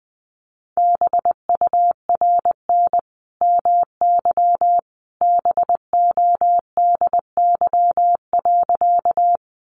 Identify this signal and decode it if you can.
Morse code, heard as a keyed tone, about 20 wpm, 700 Hz. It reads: BURN MY BODY.
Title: Sentences from Top 1300 Words 20wpm